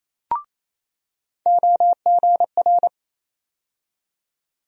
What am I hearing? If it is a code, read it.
Morse code, heard as a keyed tone, about 28 wpm, 700 Hz. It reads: OGR
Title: USA Call Sign Postfixes - 3 Letters 28wpm